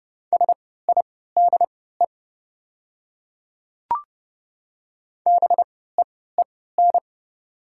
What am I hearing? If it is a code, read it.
Morse code, heard as a keyed tone, about 29 wpm, spaced as for 17 wpm, 700 Hz. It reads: SIDE BEEN
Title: 3x Character Spacing - Top 500 Words - 30wpm